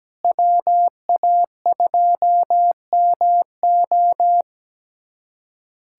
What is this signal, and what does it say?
Morse code, heard as a keyed tone, about 17 wpm, 700 Hz. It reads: WA2MO
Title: USA Callsign DE Callsign - 5 Characters - 17wpm